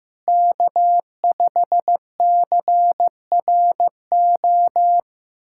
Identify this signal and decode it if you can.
Morse code, heard as a keyed tone, about 15 wpm, 700 Hz. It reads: K5CRO
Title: USA Callsign DE Callsign 15wpm